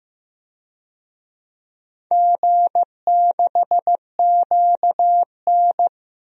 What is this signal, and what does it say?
Morse code, heard as a keyed tone, about 15 wpm, 700 Hz. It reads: G6QN
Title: DXCC-100 Call Signs - 4 Characters - Repeated 3x - 15wpm